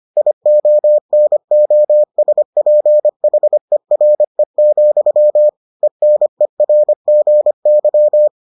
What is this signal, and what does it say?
Morse code, heard as a keyed tone, about 25 wpm, 600 Hz. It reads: IONOSPHERE, ENERGY